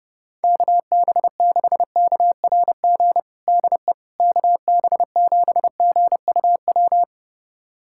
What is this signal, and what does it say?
Morse code, heard as a keyed tone, about 30 wpm, 700 Hz. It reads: KB6KRG DE KB7GUW
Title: USA Callsign DE Callsign - 6 Characters - 30wpm